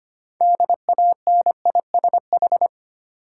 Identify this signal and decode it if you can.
Morse code, heard as a keyed tone, about 25 wpm, 700 Hz. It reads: DANISH
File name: practice / 25wpm / Difficult Words with Hints - 25wpm